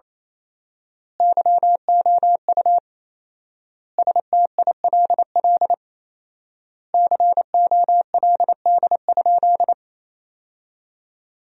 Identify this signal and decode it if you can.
Morse code, heard as a keyed tone, about 28 wpm, 700 Hz. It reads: YOU STILL COLD?
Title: 4x Word-Spacing - Sentences from Top 400 Words - 28wpm